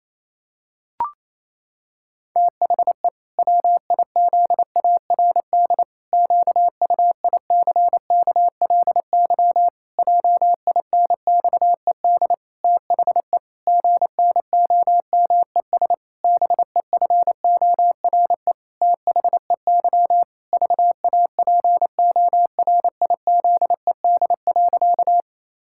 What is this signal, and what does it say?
Morse code, heard as a keyed tone, about 28 wpm, 700 Hz. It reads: THE WIZARD QUICKLY JINXED THE GNOMES BEFORE THEY VAPORIZED.
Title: Pangram Challenge v2 - 28wpm